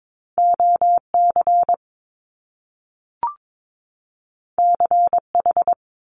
Morse code, heard as a keyed tone, about 22 wpm, 700 Hz. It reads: OC CH
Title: Top Two Letter Combinations in Words 22wpm